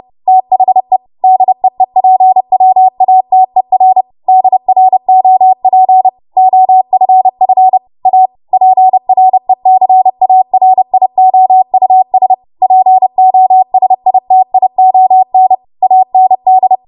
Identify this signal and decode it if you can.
Morse code, heard as a keyed tone, about 30 wpm, 750 Hz. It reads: THE DEEPWATER DROP OFF A PRECARIOUS POSITION AND
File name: Morse / 30WPM